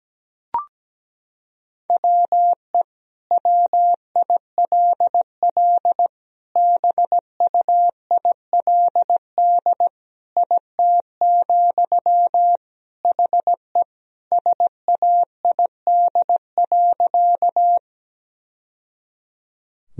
Morse code, heard as a keyed tone, about 17 wpm, 700 Hz. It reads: WE WILL BUILD IT, HE SAID.
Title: Sentences from Top 1000 Words - Review Part 1 - 17wpm